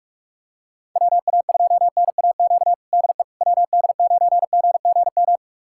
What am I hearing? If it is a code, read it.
Morse code, heard as a keyed tone, about 42 wpm, 700 Hz. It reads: WA2NAQ DE WD9GGM
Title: USA Callsign DE Callsign - 6 Characters - 45wpm